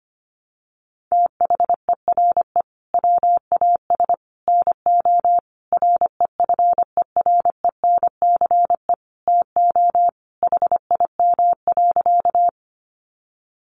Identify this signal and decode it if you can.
Morse code, heard as a keyed tone, about 25 wpm, 700 Hz. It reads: THERE WAS NO REFERENCE TO HIM.
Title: Sentences from Top 1200 Words 25wpm